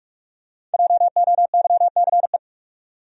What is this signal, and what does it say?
Morse code, heard as a keyed tone, about 45 wpm, 700 Hz. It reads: JOYCE
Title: USA Common Names - Max 5 Letters - 45wpm